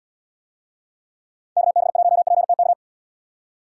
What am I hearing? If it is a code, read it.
Morse code, heard as a keyed tone, about 72 wpm, 700 Hz. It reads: KB1PTK